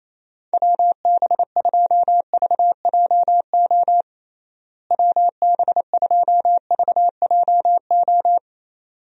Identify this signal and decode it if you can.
Morse code, heard as a keyed tone, about 28 wpm, 700 Hz. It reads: WB2VJO WB2VJO
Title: USA Call Signs - 6 Characters - Repeated 2x - 28wpm